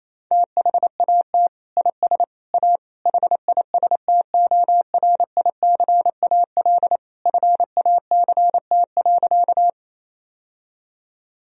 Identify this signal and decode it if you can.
Morse code, heard as a keyed tone, about 28 wpm, 700 Hz. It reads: THAT IS A HISTORICAL FACT.
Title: Sentences from Top 1000 Words - Review Part 4 - 28wpm